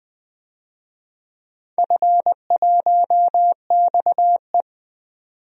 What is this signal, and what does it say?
Morse code, heard as a keyed tone, about 20 wpm, 700 Hz. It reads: F1XE
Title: DXCC-100 Call Signs - 4 Characters - 20wpm